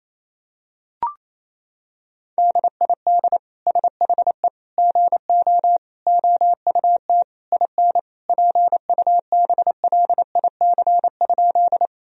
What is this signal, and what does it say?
Morse code, heard as a keyed tone, about 27 wpm, 700 Hz. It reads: DID SHE GO OUT IN PUBLIC?
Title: Sentences from Top 1000 Words - Review Part 3 - 28wpm